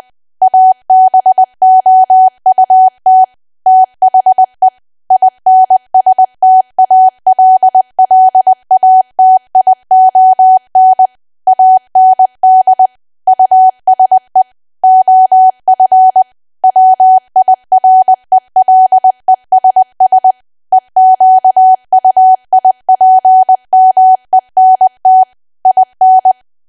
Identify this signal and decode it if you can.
Morse code, heard as a keyed tone, about 20 wpm, 750 Hz. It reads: ABOUT THE INSTALLATION AND USE OF WIRELESS EQUIPMENT IN
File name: cwa / W1AW-Level3 / 20WPM